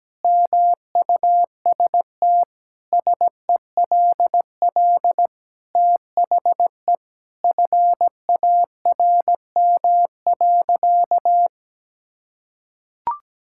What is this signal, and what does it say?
Morse code, heard as a keyed tone, about 17 wpm, 700 Hz. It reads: MUST SELL THE FARM.
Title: Sentences from Top 1000 Words - Review Part 6 - 17wpm